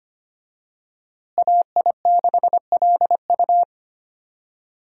Morse code, heard as a keyed tone, about 25 wpm, 700 Hz. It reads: AI6LU
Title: USA Call Signs - 5 Characters - Repeated 3x - 25wpm